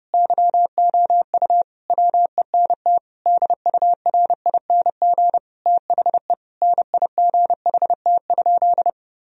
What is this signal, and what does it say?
Morse code, heard as a keyed tone, about 30 wpm, 700 Hz. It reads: YOU WENT DURING THE NIGHT?
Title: Sentences from Top 1000 Words - Review Part 2 - 30wpm